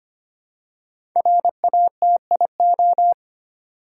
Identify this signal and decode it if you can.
Morse code, heard as a keyed tone, about 25 wpm, 700 Hz. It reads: RATIO
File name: 5 Letter Words - First 3 Letters Given - 25wpm